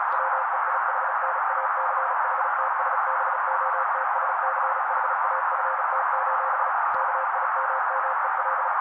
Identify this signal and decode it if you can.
Morse code, heard as a keyed tone, about 35 wpm, 530 Hz. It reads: WEIHNACHTSBOTSCHAFT: KIMMEL
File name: daily / 35wpm noise